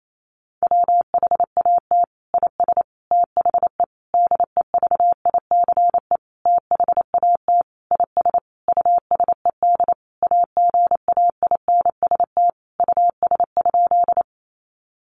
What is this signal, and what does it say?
Morse code, heard as a keyed tone, about 28 wpm, 700 Hz. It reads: WHAT IS THE DEVICE THAT IS USED AGAINST US?
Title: Sentences from Top 1500 Words 28wpm